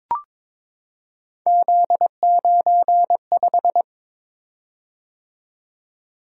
Morse code, heard as a keyed tone, about 23 wpm, 700 Hz. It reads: Z95
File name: practice / 22wpm / Three Letter-Number Combinations - 22wpm